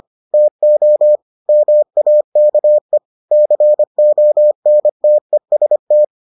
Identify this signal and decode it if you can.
Morse code, heard as a keyed tone, about 25 wpm, 600 Hz. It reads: TO MAKE CONTEST